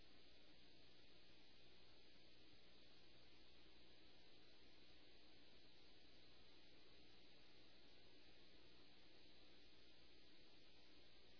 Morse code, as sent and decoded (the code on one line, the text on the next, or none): none